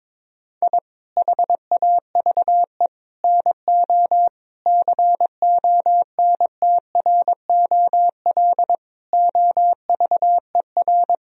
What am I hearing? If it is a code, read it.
Morse code, heard as a keyed tone, about 22 wpm, 700 Hz. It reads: I HAVE NO CONTROL OVER
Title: Sentences from Top 1000 Words - Review Part 3 - 22wpm